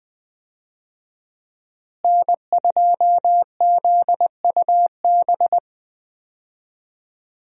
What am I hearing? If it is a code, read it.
Morse code, heard as a keyed tone, about 20 wpm, 700 Hz. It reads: N2ZUB
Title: USA Call Signs - 5 Characters - Repeated 3x - 20wpm